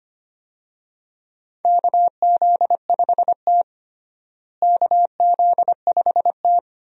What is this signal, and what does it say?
Morse code, heard as a keyed tone, about 25 wpm, 700 Hz. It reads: KZ5T KZ5T
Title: USA Call Signs - 4 Characters - Repeated 2x - 25wpm